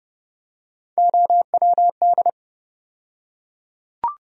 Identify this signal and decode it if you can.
Morse code, heard as a keyed tone, about 29 wpm, 700 Hz. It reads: OWD
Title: USA Call Sign Postfixes - 3 Letters 30wpm